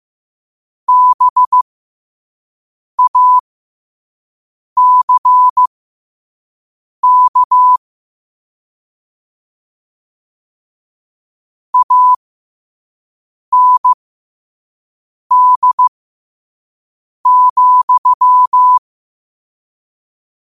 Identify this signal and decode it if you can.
Morse code, heard as a keyed tone, about 15 wpm, spaced as for 5 wpm, 1000 Hz. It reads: BACK AND,